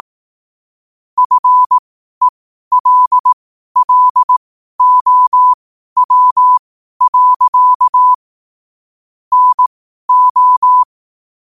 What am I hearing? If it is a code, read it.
Morse code, heard as a keyed tone, about 18 wpm, spaced as for 12 wpm, 1000 Hz. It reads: FELLOW. NO
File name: qotdfiles / 12wpm